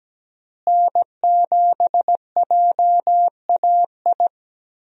Morse code, heard as a keyed tone, about 17 wpm, 700 Hz. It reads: N7JAI
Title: USA Call Signs - 5 Characters - Repeated 3x - 17wpm